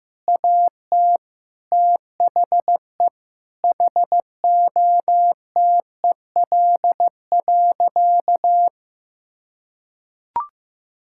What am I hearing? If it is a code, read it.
Morse code, heard as a keyed tone, about 15 wpm, 700 Hz. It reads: AT THE HOTEL.
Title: Sentences from Top 1600 Words 15wpm